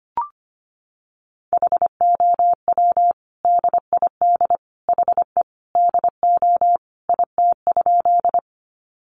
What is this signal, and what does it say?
Morse code, heard as a keyed tone, about 25 wpm, 700 Hz. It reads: HOW DID HE DO IT?
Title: Sentences from Top 100 Words 25wpm